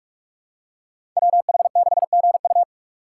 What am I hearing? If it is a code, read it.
Morse code, heard as a keyed tone, about 46 wpm, 700 Hz. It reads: WH6GU
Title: USA Call Signs - 5 Characters - 4x Character Spacing - 45wpm